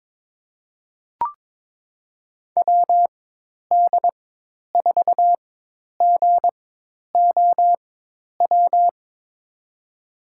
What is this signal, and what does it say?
Morse code, heard as a keyed tone, about 22 wpm, spaced as for 10 wpm, 700 Hz. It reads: WD4GOW
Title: USA Call Signs - 6 Characters - 4x Character Spacing - 22wpm